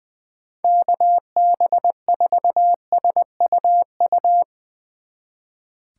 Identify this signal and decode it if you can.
Morse code, heard as a keyed tone, about 22 wpm, 700 Hz. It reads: KB4SUU
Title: Speed Racing - USA 2x3 Call Signs 20wpm